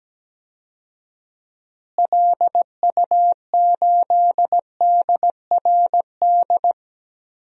Morse code, heard as a keyed tone, about 17 wpm, 700 Hz. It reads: LU8DRD
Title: DXCC-100 Call Signs - 6 Characters - Repeated 3x - 17wpm